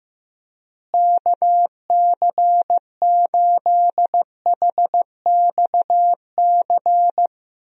Morse code, heard as a keyed tone, about 15 wpm, 700 Hz. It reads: KC8HXC
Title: USA Call Signs - 6 Characters - Repeated 2x - 15wpm